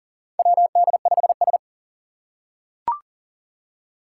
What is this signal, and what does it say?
Morse code, heard as a keyed tone, about 40 wpm, 700 Hz. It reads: WD5S
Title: Four Challenging Letter-Number Combinations 40wpm